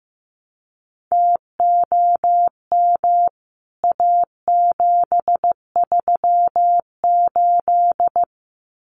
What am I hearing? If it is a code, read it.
Morse code, heard as a keyed tone, about 15 wpm, 700 Hz. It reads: TOM A738
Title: CWT Exchange 15wpm